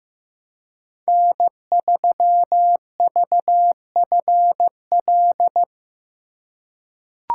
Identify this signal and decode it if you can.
Morse code, heard as a keyed tone, about 15 wpm, 700 Hz. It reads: N3VFL
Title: USA Call Signs - 5 Characters - 2x Character Spacing - 15wpm